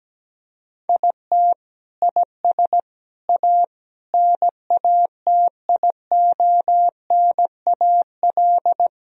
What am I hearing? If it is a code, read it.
Morse code, heard as a keyed tone, about 17 wpm, 700 Hz. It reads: IT IS A NATIONAL
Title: Sentences from Top 1000 Words - Review Part 4 - 17wpm